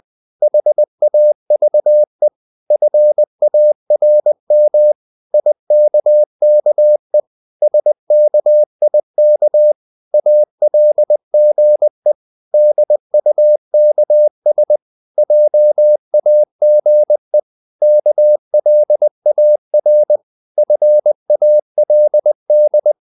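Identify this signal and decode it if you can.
Morse code, heard as a keyed tone, about 20 wpm, 600 Hz. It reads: HAVE FARM IKKE SKIK ALGE DUKS JAGE KLAR FALD
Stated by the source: Morse code practice of short Danish words at 20 wpm, from CW med Gnister.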